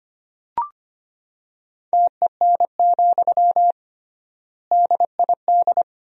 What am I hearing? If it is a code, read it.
Morse code, heard as a keyed tone, about 25 wpm, 700 Hz. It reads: TEN, DID
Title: 3x Word-Spacing - Sentences from Top 400 Words - 25wpm